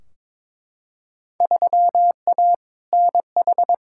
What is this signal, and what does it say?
Morse code, heard as a keyed tone, about 22 wpm, 700 Hz. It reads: 3A NH
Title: Field Day Exchanges - 22wpm